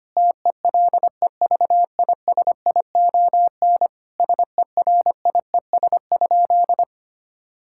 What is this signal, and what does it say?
Morse code, heard as a keyed tone, about 25 wpm, 700 Hz. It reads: TELEVISION SERIES?
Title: Sentences from Top 1000 Words - Review Part 5 - 25wpm